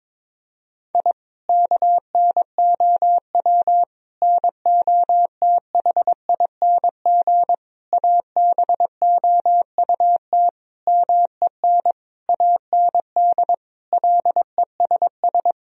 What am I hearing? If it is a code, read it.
Morse code, heard as a keyed tone, about 21 wpm, 700 Hz. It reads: I KNOW NOTHING ABOUT MEN AND LESS
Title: Sentences from Top 1000 Words - Review Part 2 - 22wpm